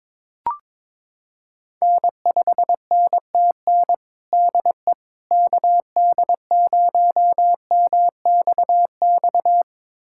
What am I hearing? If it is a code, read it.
Morse code, heard as a keyed tone, about 22 wpm, 700 Hz. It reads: N5NTN DE KD0MXX
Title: USA Callsign DE Callsign 22wpm